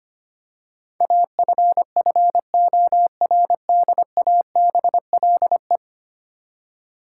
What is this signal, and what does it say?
Morse code, heard as a keyed tone, about 25 wpm, 700 Hz. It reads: AFFORDABLE
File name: Ten Letter Words - 25wpm